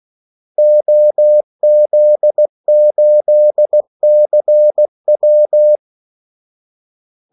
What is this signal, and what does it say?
Morse code, heard as a keyed tone, about 16 wpm, 600 Hz. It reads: OZ8CW
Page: Kaldesignaler 16 wpm